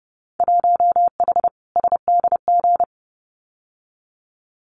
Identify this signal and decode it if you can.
Morse code, heard as a keyed tone, about 30 wpm, 700 Hz. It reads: 1H SDG
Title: Winter Field Day Exchanges 30wpm